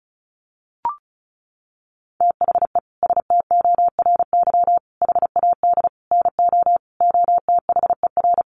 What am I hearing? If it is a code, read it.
Morse code, heard as a keyed tone, about 34 wpm, 700 Hz. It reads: THE STORY HAD NO OTHER
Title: Sentences from Top 1200 Words 35wpm